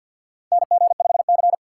Morse code, heard as a keyed tone, about 46 wpm, 700 Hz. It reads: NG5C
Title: USA Call Signs - 4 Characters - 2x Character Spacing - 50wpm